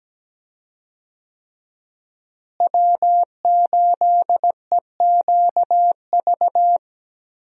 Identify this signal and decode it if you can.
Morse code, heard as a keyed tone, about 17 wpm, 700 Hz. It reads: W8EQV
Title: USA Call Signs - 5 Characters - Repeated 3x - 17wpm